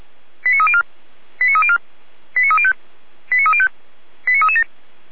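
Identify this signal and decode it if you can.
Five-tone selective call (ZVEI), heard as quick runs of five tones, tones of about 70 ms. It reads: 89304 89304 89305 89305 89337
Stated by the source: ZVEI five-tone call 89094 received on an SDR.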